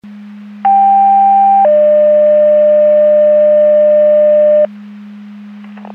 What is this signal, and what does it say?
Two-tone page, heard as a one-second tone, then a three-second tone, about 790 then 600 Hz, with noise. It is heard from 0.6 to 4.7 s.